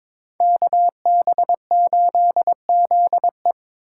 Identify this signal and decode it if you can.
Morse code, heard as a keyed tone, about 27 wpm, 700 Hz. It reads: KB8ZE
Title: Speed Racing - USA 2x2 Call Signs 22wpm